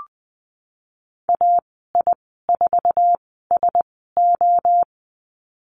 Morse code, heard as a keyed tone, about 20 wpm, spaced as for 14 wpm, 700 Hz. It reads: AI4SO
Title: 055 Lesson 10g - Letter r - Call Signs 20wpm